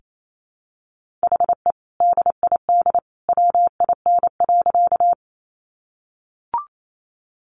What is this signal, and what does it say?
Morse code, heard as a keyed tone, about 28 wpm, 700 Hz. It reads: HE DID WIN.
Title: Sentences from Top 1500 Words 28wpm